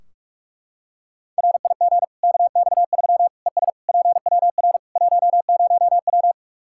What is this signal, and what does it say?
Morse code, heard as a keyed tone, about 43 wpm, 700 Hz. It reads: RIG KX3 ES PWR 10W
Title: Practice QSO - Exchange 2 - Info - 45wpm